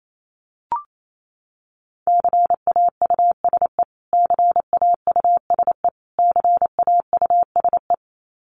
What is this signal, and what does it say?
Morse code, heard as a keyed tone, about 28 wpm, 700 Hz. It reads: CAUSE CAUSE CAUSE
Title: Top 500 Words Repeated 28wpm